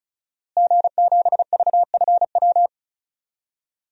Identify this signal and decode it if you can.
Morse code, heard as a keyed tone, about 36 wpm, 700 Hz. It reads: G7VFW
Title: DXCC-100 Call Signs - 5 Characters - Repeated 3x - 35wpm